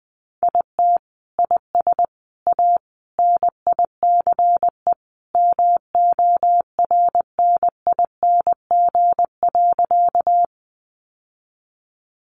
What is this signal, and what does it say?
Morse code, heard as a keyed tone, about 20 wpm, 700 Hz. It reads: IT IS A NICE MORNING.
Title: Sentences from Top 1100 Words 20wpm